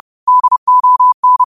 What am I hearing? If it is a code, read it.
Morse code, heard as a keyed tone, about 30 wpm, 1000 Hz. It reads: DON
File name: qotdfiles / 30wpm